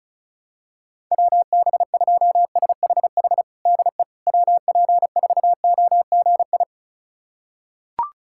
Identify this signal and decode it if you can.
Morse code, heard as a keyed tone, about 36 wpm, 700 Hz. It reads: WB2SHH DE WP4OGI
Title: USA Callsign DE Callsign - 6 Characters - 35wpm